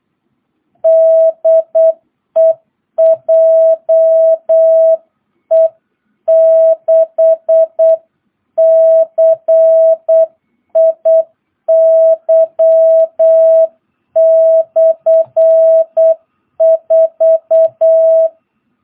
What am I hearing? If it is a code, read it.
Morse code, heard as a keyed tone, about 8 wpm, 650 Hz. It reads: DE JE6CIY/4
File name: jh8cbh / 2022cw